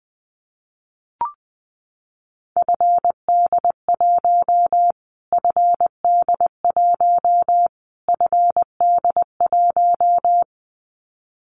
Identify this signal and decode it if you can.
Morse code, heard as a keyed tone, about 20 wpm, 700 Hz. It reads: FD1 FD1 FD1
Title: Callsign Prefixes for Top 25 DXCC - Repeated 3x - 20wpm